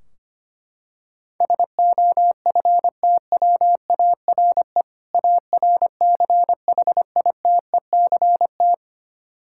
Morse code, heard as a keyed tone, about 25 wpm, 700 Hz. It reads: SOFTWARE ARCHITECT
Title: Practice QSO - Exchange 3 - More Info - 25wpm